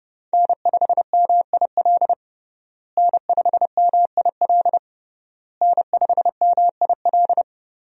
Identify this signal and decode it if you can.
Morse code, heard as a keyed tone, about 30 wpm, 700 Hz. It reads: N5MIL N5MIL N5MIL
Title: USA Call Signs - 5 Characters - Repeated 3x - 30wpm